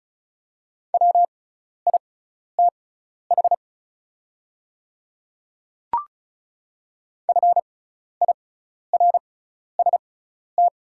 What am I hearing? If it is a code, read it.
Morse code, heard as a keyed tone, about 36 wpm, spaced as for 12 wpm, 700 Hz. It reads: WITH FIRST
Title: 6x Character Spacing - Top 100 Words - 35wpm